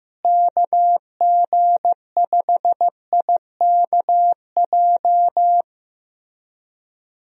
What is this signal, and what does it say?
Morse code, heard as a keyed tone, about 15 wpm, 700 Hz. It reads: KG5IKJ
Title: USA Call Signs - 6 Characters - Repeated 2x - 15wpm